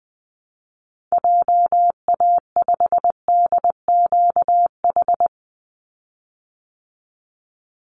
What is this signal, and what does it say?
Morse code, heard as a keyed tone, about 20 wpm, 700 Hz. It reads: JA5DQH
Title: Call Signs Repeated 3x - 20wpm